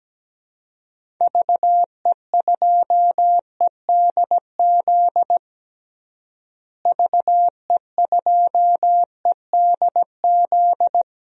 Morse code, heard as a keyed tone, about 17 wpm, 700 Hz. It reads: VE2EDZ VE2EDZ
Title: DXCC-100 Call Signs - 6 Characters - Repeated 3x - 17wpm